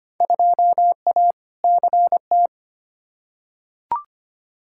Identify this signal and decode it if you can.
Morse code, heard as a keyed tone, about 25 wpm, 700 Hz. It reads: 2A CT
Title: Field Day Exchanges - 25wpm